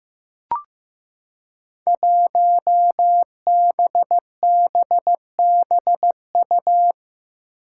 Morse code, heard as a keyed tone, about 16 wpm, 700 Hz. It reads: 1BBBU
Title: Five Challenging Letter-Number Combinations 15wpm